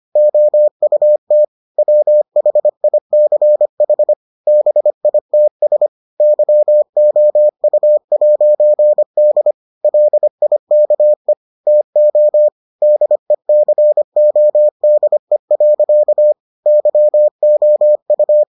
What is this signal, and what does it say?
Morse code, heard as a keyed tone, about 25 wpm, 600 Hz. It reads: OUT WHICH BITS YOU'D LIKE TO DECODE. YOU